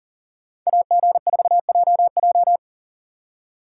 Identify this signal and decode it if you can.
Morse code, heard as a keyed tone, about 48 wpm, 700 Hz. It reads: AG4JJ
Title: Speed Racing - USA 2x2 Call Signs 40wpm